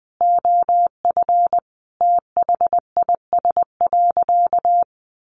Morse code, heard as a keyed tone, about 20 wpm, 700 Hz. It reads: OF THIS.
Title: Sentences from Top 1100 Words 20wpm